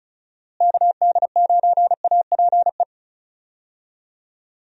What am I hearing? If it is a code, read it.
Morse code, heard as a keyed tone, about 34 wpm, 700 Hz. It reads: KD9APE
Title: USA Call Signs - 6 Characters - Repeated 2x - 35wpm